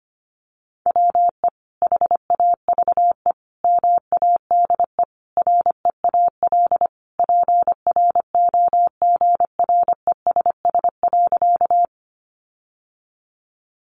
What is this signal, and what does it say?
Morse code, heard as a keyed tone, about 25 wpm, 700 Hz. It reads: WE HAVE MADE REAL PROGRESS.
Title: Sentences from Top 1400 Words 25wpm